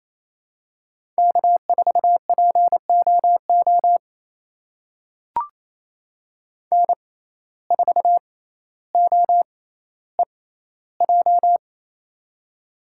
Morse code, heard as a keyed tone, about 28 wpm, spaced as for 14 wpm, 700 Hz. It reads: K4POO N4OEJ
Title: USA Call Signs - 5 Characters - 6x Character Spacing - 28wpm